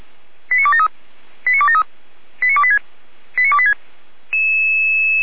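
Five-tone selective call (ZVEI), heard as quick runs of five tones, tones of about 80 ms. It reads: 89273 89273 89286 89286
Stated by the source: ZVEI five-tone selective call received on an SDR.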